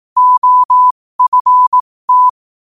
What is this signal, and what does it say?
Morse code, heard as a keyed tone, about 18 wpm, spaced as for 15 wpm, 1000 Hz. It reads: OFT
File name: qotdfiles / 15wpm